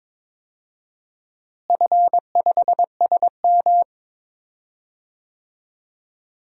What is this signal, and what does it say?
Morse code, heard as a keyed tone, about 22 wpm, 700 Hz. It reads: F5SM
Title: DXCC-100 Call Signs - 4 Characters - Repeated 2x - 22wpm